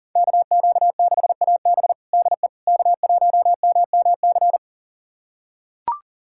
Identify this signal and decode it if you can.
Morse code, heard as a keyed tone, about 40 wpm, 700 Hz. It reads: KQ6AB DE K1MMC
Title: USA Callsign DE Callsign - 5 Characters - 40wpm